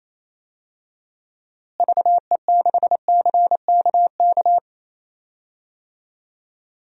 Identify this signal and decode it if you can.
Morse code, heard as a keyed tone, about 28 wpm, 700 Hz. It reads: VE6CKK
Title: DXCC-100 Call Signs - 6 Characters - Repeated 3x - 28wpm